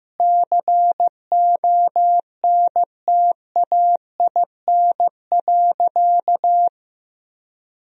Morse code, heard as a keyed tone, about 15 wpm, 700 Hz. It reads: CONTAIN.
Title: Sentences from Top 1000 Words - Review Part 2 - 15wpm